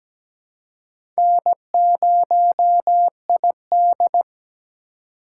Morse code, heard as a keyed tone, about 17 wpm, 700 Hz. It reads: N0ID
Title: USA Call Signs - 2x Character Spacing - 17wpm